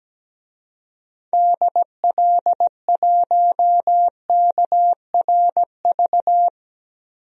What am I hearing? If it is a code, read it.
Morse code, heard as a keyed tone, about 17 wpm, 700 Hz. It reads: DL1KRV
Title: DXCC-100 Call Signs - 6 Characters - 17wpm